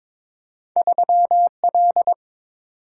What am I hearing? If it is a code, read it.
Morse code, heard as a keyed tone, about 23 wpm, 700 Hz. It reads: 3L
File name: practice / 22wpm / Two Letter-Number Combinations - 22wpm